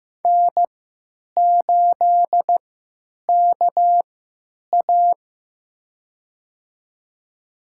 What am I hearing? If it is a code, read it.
Morse code, heard as a keyed tone, about 15 wpm, spaced as for 9 wpm, 700 Hz. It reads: N8KA
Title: USA Call Signs - 4 Characters - 3x Character Spacing - 15wpm